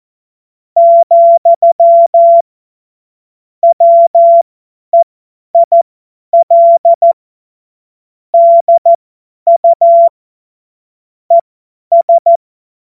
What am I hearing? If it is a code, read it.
Morse code, heard as a keyed tone, about 14 wpm, spaced as for 10 wpm, 680 Hz. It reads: , WEIL DU ES
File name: Morsekurs / Witze / Witz 13. September 2022 50 F70